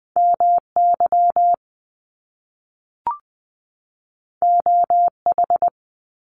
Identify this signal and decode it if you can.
Morse code, heard as a keyed tone, about 20 wpm, 700 Hz. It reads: MY OH
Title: Top Two Letter Combinations in Words 20wpm